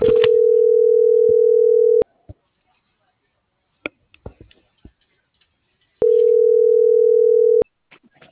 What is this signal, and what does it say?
Ringback tone (North America), 440 + 480 Hz.
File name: CENTRAL GOMSA / out garcia